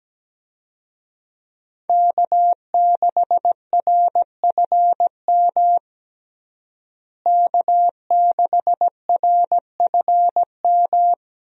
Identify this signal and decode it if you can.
Morse code, heard as a keyed tone, about 17 wpm, 700 Hz. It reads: K6RFM K6RFM
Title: USA Call Signs - 5 Characters - Repeated 2x - 17wpm